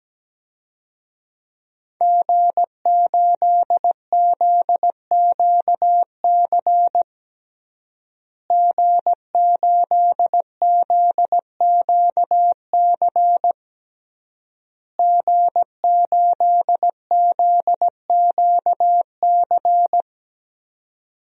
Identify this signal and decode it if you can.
Morse code, heard as a keyed tone, about 17 wpm, 700 Hz. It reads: G8ZQC G8ZQC G8ZQC
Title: DXCC-100 Call Signs - 5 Characters - Repeated 3x - 17wpm